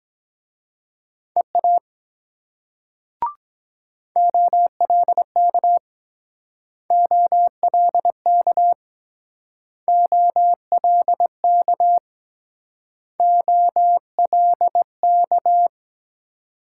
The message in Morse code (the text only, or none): EA OLK OLK OLK OLK